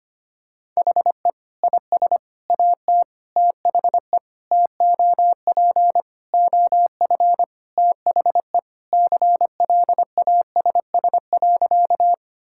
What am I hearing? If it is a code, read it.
Morse code, heard as a keyed tone, about 25 wpm, 700 Hz. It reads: HE IS AT THE TOP OF THE CLASS.
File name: practice / 25wpm / Sentences with 3-Word Phrases - 25wpm